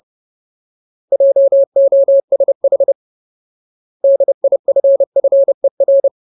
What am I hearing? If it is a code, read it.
Morse code, heard as a keyed tone, about 30 wpm, 550 Hz. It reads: JOSH DIFFER
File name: Ultimate Word List / 30WPM Ultimate